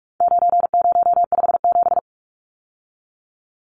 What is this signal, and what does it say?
Morse code, heard as a keyed tone, about 44 wpm, 700 Hz. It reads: 9057
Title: Numbers - 4-Digit 45wpm